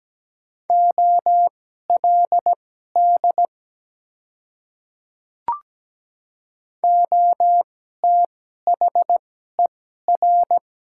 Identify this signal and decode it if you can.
Morse code, heard as a keyed tone, about 17 wpm, spaced as for 12 wpm, 700 Hz. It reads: OLD OTHER
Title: 2x Character Spacing - Top 200 Words - 17wpm